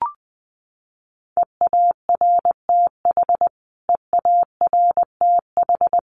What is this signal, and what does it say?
Morse code, heard as a keyed tone, about 20 wpm, 700 Hz. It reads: EARTH EARTH
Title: Top 500 Words Repeated 20wpm